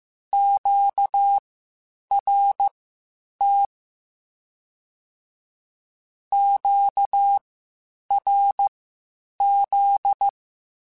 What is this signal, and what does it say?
Morse code, heard as a keyed tone, about 15 wpm, spaced as for 8 wpm, 800 Hz. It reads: QRT QRZ
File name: (spaced) / Q-Calls Spaced-mp3s / Q Calls Twice sp @13